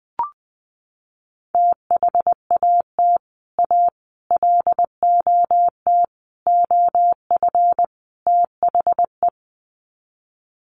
Morse code, heard as a keyed tone, about 20 wpm, 700 Hz. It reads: THAT A LOT OF THE
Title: Sets of 5 Words Encore2 20wpm